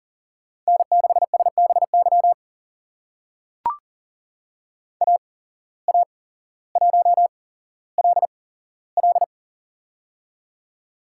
Morse code, heard as a keyed tone, about 43 wpm, spaced as for 17 wpm, 700 Hz. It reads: N6SBY AA1LL
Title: USA Call Signs - 5 Characters - 8x Character Spacing - 40wpm